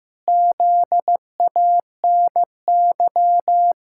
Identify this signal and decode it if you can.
Morse code, heard as a keyed tone, about 15 wpm, 700 Hz. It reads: ZANY
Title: Pangram Challenge v2 - 15wpm